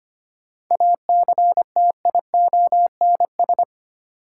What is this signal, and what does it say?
Morse code, heard as a keyed tone, about 25 wpm, 700 Hz. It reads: ACTIONS
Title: Sentences from Top 1000 Words - Review Part 6 - New Word Drill - 25wpm